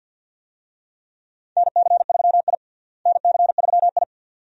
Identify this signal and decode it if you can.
Morse code, heard as a keyed tone, about 49 wpm, 700 Hz. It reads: NC3I NC3I
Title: USA Call Signs - 4 Characters - Repeated 2x - 50wpm